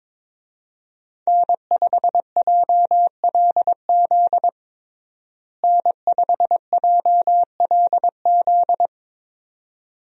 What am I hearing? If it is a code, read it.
Morse code, heard as a keyed tone, about 22 wpm, 700 Hz. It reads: N5JLZ N5JLZ
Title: USA Call Signs - 5 Characters - Repeated 2x - 22wpm